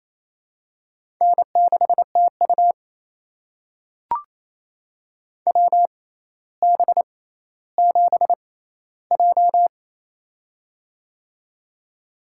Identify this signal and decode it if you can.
Morse code, heard as a keyed tone, about 28 wpm, spaced as for 14 wpm, 700 Hz. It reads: N6TU WB7J
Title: USA Call Signs - 4 Characters - 6x Character Spacing - 28wpm